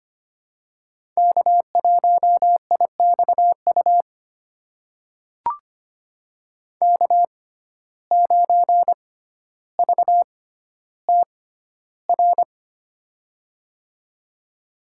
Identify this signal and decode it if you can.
Morse code, heard as a keyed tone, about 25 wpm, spaced as for 13 wpm, 700 Hz. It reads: K1IXU K9VTR
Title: USA Call Signs - 5 Characters - 6x Character Spacing - 25wpm